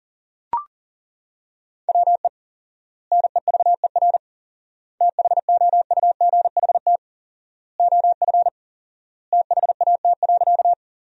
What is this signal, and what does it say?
Morse code, heard as a keyed tone, about 37 wpm, 700 Hz. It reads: WE NEVER THOUGHT OF THAT.
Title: 4x Word-Spacing - Sentences from Top 300 Words - 40wpm